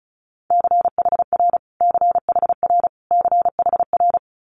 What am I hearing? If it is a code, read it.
Morse code, heard as a keyed tone, about 36 wpm, 700 Hz. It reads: CHR CHR CHR
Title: Multi-letter Phonemes Repeated 3x 35wpm